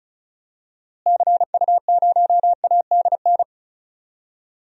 Morse code, heard as a keyed tone, about 36 wpm, 700 Hz. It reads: CU0ADN
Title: DXCC-100 Call Signs - 6 Characters - Repeated 3x - 35wpm